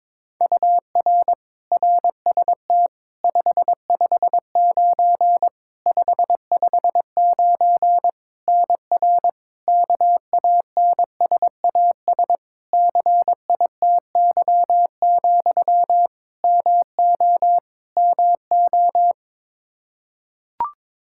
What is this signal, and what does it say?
Morse code, heard as a keyed tone, about 21 wpm, 700 Hz. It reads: UR RST 559 559 NR KANSAS CITY, MO MO
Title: K1USN SST QSO Practice 22wpm